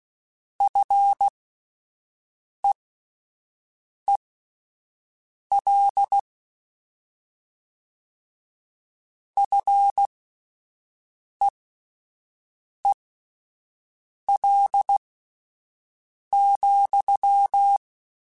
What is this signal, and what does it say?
Morse code, heard as a keyed tone, about 16 wpm, spaced as for 5 wpm, 800 Hz. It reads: FEEL FEEL,